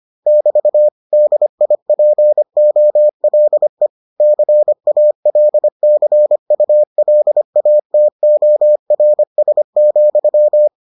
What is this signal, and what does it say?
Morse code, heard as a keyed tone, about 25 wpm, 600 Hz. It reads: <BT> DIPOLE CALCULATORS,